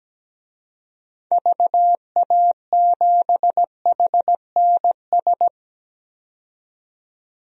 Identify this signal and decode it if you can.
Morse code, heard as a keyed tone, about 17 wpm, 700 Hz. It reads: VA7HNS
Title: DXCC-100 Call Signs - 6 Characters - 17wpm